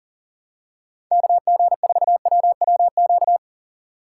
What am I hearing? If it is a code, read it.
Morse code, heard as a keyed tone, about 38 wpm, 700 Hz. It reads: KG4WWQ
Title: USA Call Signs - 6 Characters - 6x Character Spacing - 40wpm